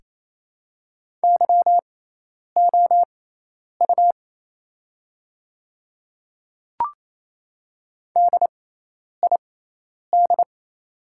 Morse code, heard as a keyed tone, about 28 wpm, spaced as for 10 wpm, 700 Hz. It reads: YOU DID
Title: 6x Character Spacing - Top 200 Words - 28wpm